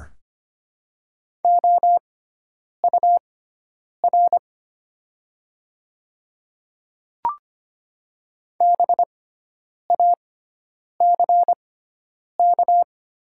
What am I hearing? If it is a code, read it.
Morse code, heard as a keyed tone, about 25 wpm, spaced as for 9 wpm, 700 Hz. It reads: OUR BACK
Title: 6x Character Spacing - Top 200 Words - 25wpm